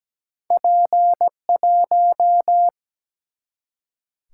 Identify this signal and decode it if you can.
Morse code, heard as a keyed tone, about 17 wpm, 700 Hz. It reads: P1
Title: Single-Letter Single-Number 17wpm